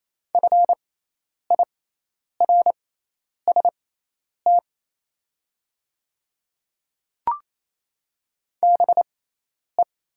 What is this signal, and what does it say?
Morse code, heard as a keyed tone, about 28 wpm, spaced as for 10 wpm, 700 Hz. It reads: FIRST BE
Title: 6x Character Spacing - Top 100 Words - 28wpm